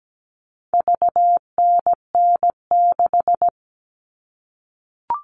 Morse code, heard as a keyed tone, about 17 wpm, 700 Hz. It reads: VNN6
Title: All Cut Numbers - 4-Digit 17wpm